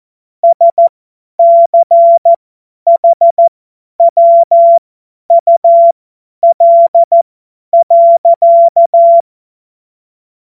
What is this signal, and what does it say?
Morse code, heard as a keyed tone, about 14 wpm, spaced as for 10 wpm, 680 Hz. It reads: SCHWUL.
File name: Morsekurs / Witze / Witz 16 Januar 2024 50 F70